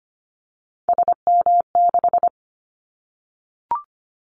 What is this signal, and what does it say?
Morse code, heard as a keyed tone, about 25 wpm, 700 Hz. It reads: SM6
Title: Callsign Prefixes for Top 25 DXCC - Repeated 3x - 25wpm